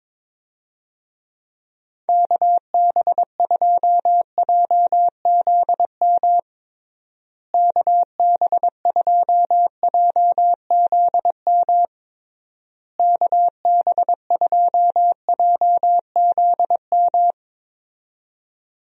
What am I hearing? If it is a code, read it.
Morse code, heard as a keyed tone, about 22 wpm, 700 Hz. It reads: KB2JZM KB2JZM KB2JZM
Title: USA Call Signs - 6 Characters - Repeated 3x - 22wpm